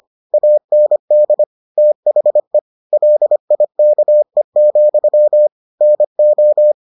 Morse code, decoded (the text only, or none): AND THE LIKE, NO